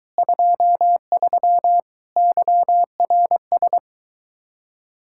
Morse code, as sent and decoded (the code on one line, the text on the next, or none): ..--- ...-- / -.-- .-. ...
23 YRS